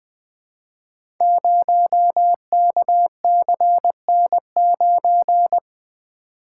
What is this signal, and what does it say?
Morse code, heard as a keyed tone, about 20 wpm, 700 Hz. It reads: 0KCN9
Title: Code Groups - 5 Letter:Numbers - 20wpm